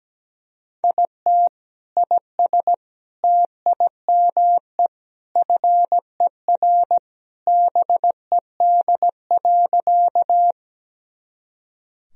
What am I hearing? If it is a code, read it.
Morse code, heard as a keyed tone, about 17 wpm, 700 Hz. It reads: IT IS TIME FER BED.
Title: Sentences from Top 500 Words and Abbreviations 17wpm